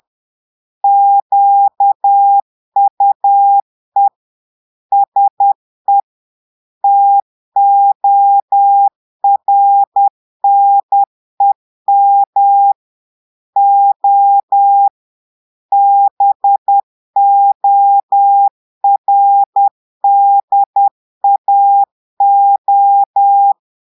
Morse code, heard as a keyed tone, about 10 wpm, 800 Hz. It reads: QUE SE TORNEM O BORDAO
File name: portugais texte en clair 2 vitesse 10 mots minutes